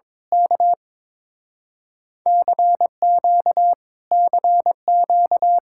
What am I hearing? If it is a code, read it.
Morse code, heard as a keyed tone, about 22 wpm, 700 Hz. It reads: K CQ CQ
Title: Speed-Racing Common QSO Phrases - 22wpm